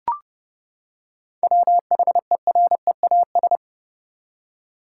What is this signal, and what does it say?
Morse code, heard as a keyed tone, about 31 wpm, 700 Hz. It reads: WHEREAS
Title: Seven Letter Words - 30wpm